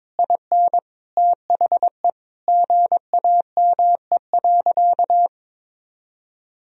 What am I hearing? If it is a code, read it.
Morse code, heard as a keyed tone, about 22 wpm, 700 Hz. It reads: IN THE GAME.
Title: Sentences from Top 1000 Words - Review Part 3 - 22wpm